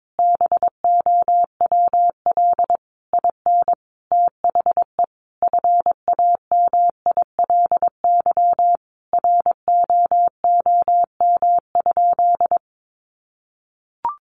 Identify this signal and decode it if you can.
Morse code, heard as a keyed tone, about 22 wpm, 700 Hz. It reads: BOWL IN THE FAMILY ROOM?
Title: Sentences from Top 1200 Words 22wpm